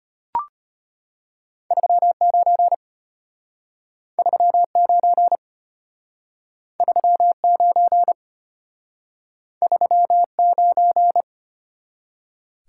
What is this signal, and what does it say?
Morse code, heard as a keyed tone, about 34 wpm, 700 Hz. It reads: 39 39 39 39